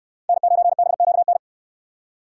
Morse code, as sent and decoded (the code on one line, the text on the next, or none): -. ----. -... --.- -.
N9BQN